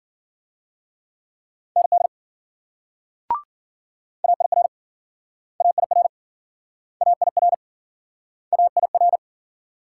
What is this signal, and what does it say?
Morse code, heard as a keyed tone, about 52 wpm, 700 Hz. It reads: ND AIR AIR AIR AIR